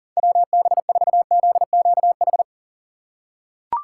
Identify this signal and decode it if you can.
Morse code, heard as a keyed tone, about 40 wpm, 700 Hz. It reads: WB4ZQH
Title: USA Call Signs - 6 Characters - 8x Character Spacing - 40wpm